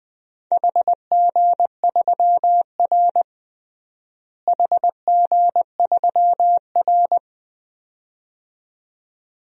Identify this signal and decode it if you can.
Morse code, heard as a keyed tone, about 21 wpm, 700 Hz. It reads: HG3R HG3R
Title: DXCC-100 Call Signs - 4 Characters - Repeated 2x - 20wpm